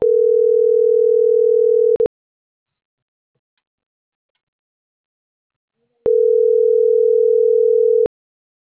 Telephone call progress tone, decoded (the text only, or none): ringback tone